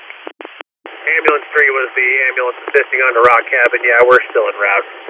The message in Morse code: none

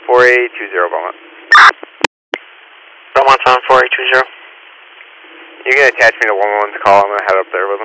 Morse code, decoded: none

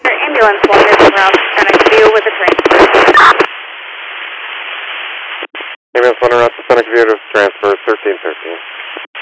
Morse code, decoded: none

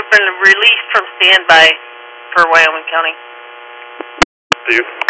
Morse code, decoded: none

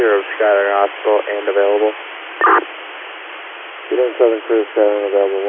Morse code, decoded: none

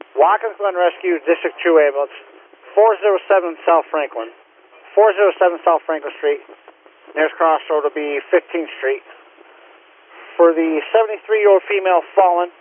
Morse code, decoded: none

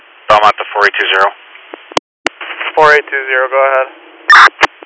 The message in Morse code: none

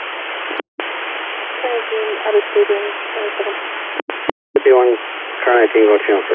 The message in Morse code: none

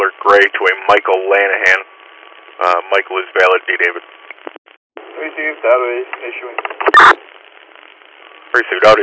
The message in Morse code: none

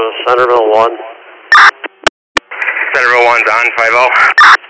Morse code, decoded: none

